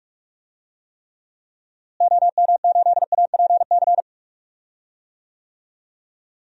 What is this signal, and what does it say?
Morse code, heard as a keyed tone, about 41 wpm, 700 Hz. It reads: OM8APC
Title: DXCC-100 Call Signs - 6 Characters - Repeated 3x - 45wpm